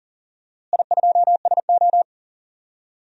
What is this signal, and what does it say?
Morse code, heard as a keyed tone, about 40 wpm, 700 Hz. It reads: I2SO
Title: DXCC-100 Call Signs - 4 Characters - Repeated 3x - 40wpm